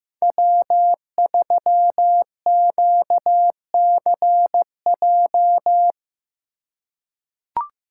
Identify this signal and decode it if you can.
Morse code, heard as a keyed tone, about 15 wpm, 700 Hz. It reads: W3QCJ
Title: USA Call Signs - 5 Characters - 2x Character Spacing - 15wpm